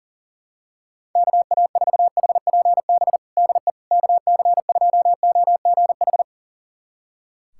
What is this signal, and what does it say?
Morse code, heard as a keyed tone, about 37 wpm, 700 Hz. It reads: KA4HPB DE KC2OGH
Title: USA Callsign DE Callsign 40wpm